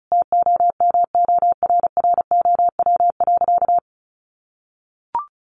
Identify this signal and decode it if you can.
Morse code, heard as a keyed tone, about 34 wpm, 700 Hz. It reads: TOMORROW.
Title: Sentences from Top 1100 Words 35wpm